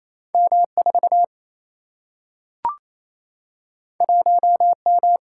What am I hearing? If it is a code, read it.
Morse code, heard as a keyed tone, about 28 wpm, 700 Hz. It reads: M4 1M
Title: Single-Letter Single-Number 28wpm